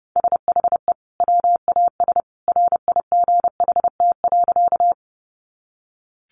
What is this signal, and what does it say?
Morse code, heard as a keyed tone, about 30 wpm, 700 Hz. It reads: SHE WAS RIGHT.
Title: Sentences from Top 200 Words 30wpm